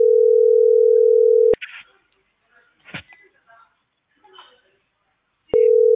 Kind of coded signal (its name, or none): telephone call progress tone